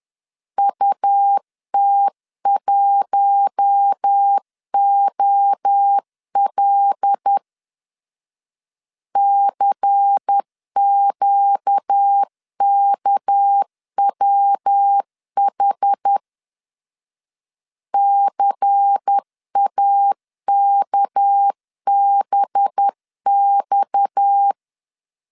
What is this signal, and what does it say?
Morse code, heard as a keyed tone, about 10 wpm, 800 Hz. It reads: UT1OL CQKWH CAKBX
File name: CW / DET / 1 6 1500C S C 10wpm 4252